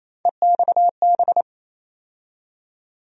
Morse code, heard as a keyed tone, about 28 wpm, 700 Hz. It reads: EXB
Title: USA Call Sign Postfixes - 3 Letters 28wpm